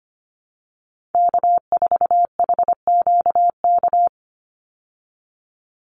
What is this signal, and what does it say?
Morse code, heard as a keyed tone, about 25 wpm, 700 Hz. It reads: K4HQK
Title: Call Signs Repeated 2x - 25wpm